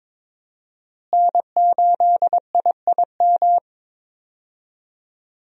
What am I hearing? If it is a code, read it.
Morse code, heard as a keyed tone, about 21 wpm, 700 Hz. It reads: N8IIM
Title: USA Call Signs - 5 Characters - Repeated 2x - 22wpm